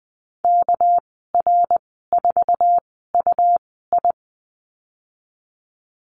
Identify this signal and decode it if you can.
Morse code, heard as a keyed tone, about 20 wpm, spaced as for 15 wpm, 700 Hz. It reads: KR4UI
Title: 207 Lesson 33g - Letter j - Call Signs 20wpm